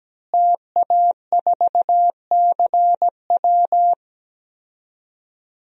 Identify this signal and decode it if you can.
Morse code, heard as a keyed tone, about 17 wpm, 700 Hz. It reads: TA4CW
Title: DXCC-100 Call Signs - 5 Characters - Repeated 2x - 17wpm